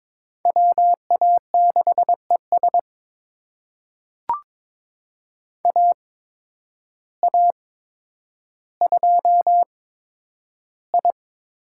Morse code, heard as a keyed tone, about 21 wpm, spaced as for 10 wpm, 700 Hz. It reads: WA6ES AA2I
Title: USA Call Signs - 5 Characters - 8x Character Spacing - 22wpm